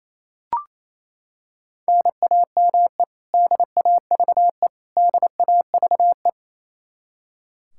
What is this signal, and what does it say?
Morse code, heard as a keyed tone, about 28 wpm, 700 Hz. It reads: NAME DAVE DAVE
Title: Practice QSO - Exchange 1 - Beginning - 28wpm